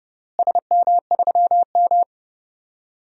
Morse code, heard as a keyed tone, about 30 wpm, 700 Hz. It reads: SM3M
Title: DXCC-100 Call Signs - 4 Characters - Repeated 3x - 30wpm